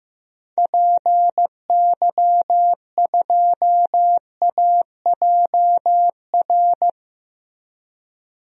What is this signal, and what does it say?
Morse code, heard as a keyed tone, about 15 wpm, 700 Hz. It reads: PY2AJR
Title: DXCC-100 Call Signs - 6 Characters - Repeated 3x - 15wpm